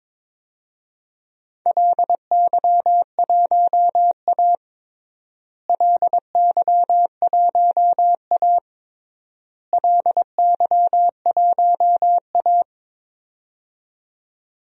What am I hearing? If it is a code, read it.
Morse code, heard as a keyed tone, about 22 wpm, 700 Hz. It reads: LY1A LY1A LY1A
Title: DXCC-100 Call Signs - 4 Characters - Repeated 3x - 22wpm